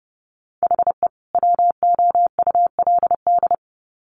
Morse code, heard as a keyed tone, about 30 wpm, 700 Hz. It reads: HE WOULD
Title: Sets of 2 Words 30wpm